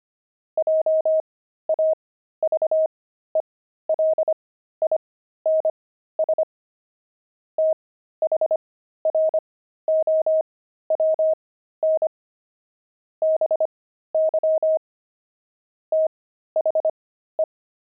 Morse code, heard as a keyed tone, about 25 wpm, spaced as for 13 wpm, 630 Hz. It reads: JAVELINS THROWN BY THE
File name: pangram-fs-13-cs-25-javelins